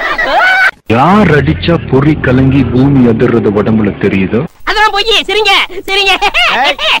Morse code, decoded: none